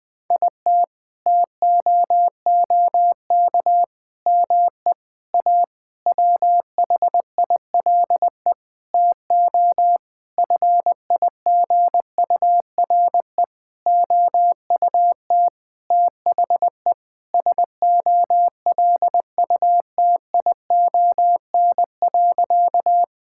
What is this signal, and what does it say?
Morse code, heard as a keyed tone, about 20 wpm, 700 Hz. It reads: IT TOOK ME A WHILE TO FIGURE OUT THE SOLUTION.
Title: Sentences with 8-Word Phrases - Spoken Beginning - 20wpm